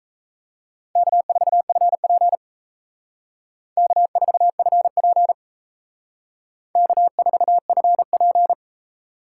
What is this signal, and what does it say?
Morse code, heard as a keyed tone, about 35 wpm, 700 Hz. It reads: K4FP K4FP K4FP